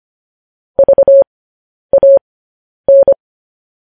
Morse code, heard as a keyed tone, about 25 wpm, spaced as for 10 wpm, 550 Hz. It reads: VAN
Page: News Headlines in Morse Code at 10 WPM